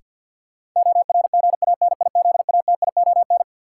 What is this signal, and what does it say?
Morse code, heard as a keyed tone, about 53 wpm, 700 Hz. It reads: ORGANIZATION
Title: Sentences from Top 1000 Words - Review Part 5 - New Word Drill - 50wpm